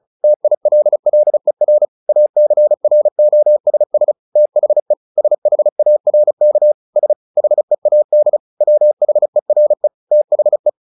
Morse code, heard as a keyed tone, about 34 wpm, 600 Hz. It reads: TILLER ACROSS THE SHARK S HEAD WHERE THE
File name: the old man and the sea 2 35wpm